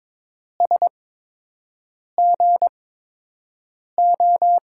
Morse code, heard as a keyed tone, about 22 wpm, spaced as for 6 wpm, 700 Hz. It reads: SGO
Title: USA Call Signs - 6 Characters - 8x Character Spacing - 22wpm